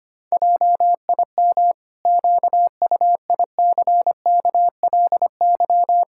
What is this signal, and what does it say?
Morse code, heard as a keyed tone, about 25 wpm, 700 Hz. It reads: JIM QUICKLY
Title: Pangram Challenge v2 - 25wpm